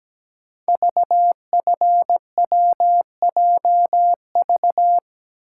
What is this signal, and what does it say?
Morse code, heard as a keyed tone, about 17 wpm, 700 Hz. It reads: VFWJV
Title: Code Groups - 5 Letters - 17wpm